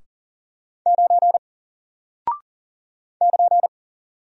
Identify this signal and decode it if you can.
Morse code, heard as a keyed tone, about 40 wpm, 700 Hz. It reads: 9 <KN>
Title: Single Character (Letter:Number:Punctuation:Prosign) Alt 40wpm